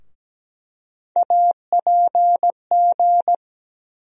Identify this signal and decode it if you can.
Morse code, heard as a keyed tone, about 17 wpm, 700 Hz. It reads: APG
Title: USA Call Sign Postfixes - 3 Letters 17wpm